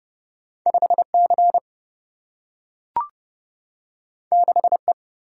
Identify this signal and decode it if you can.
Morse code, heard as a keyed tone, about 32 wpm, 700 Hz. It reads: 5 C 6 E
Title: Single-Letter Single-Number 30wpm